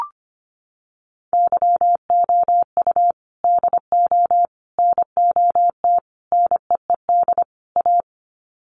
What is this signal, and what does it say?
Morse code, heard as a keyed tone, about 25 wpm, 700 Hz. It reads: YOU DO NOT NEED A
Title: Sets of 5 Words Encore 25wpm